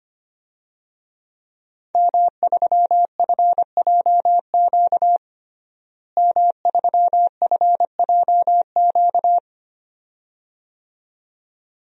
Morse code, heard as a keyed tone, about 25 wpm, 700 Hz. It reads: M3FJQ M3FJQ
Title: DXCC-100 Call Signs - 5 Characters - Repeated 2x - 25wpm